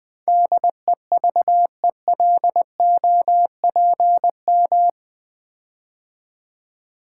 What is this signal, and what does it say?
Morse code, heard as a keyed tone, about 20 wpm, 700 Hz. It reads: DEVELOPM
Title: Build A Word - 4 Letters At A Time - 11-Letter Words - 20wpm